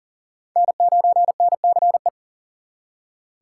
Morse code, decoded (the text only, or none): N9NCE